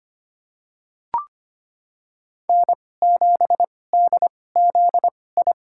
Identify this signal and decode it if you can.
Morse code, heard as a keyed tone, about 25 wpm, spaced as for 18 wpm, 700 Hz. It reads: N7DZI
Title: USA Call Signs - 5 Characters - 2x Character Spacing - 25wpm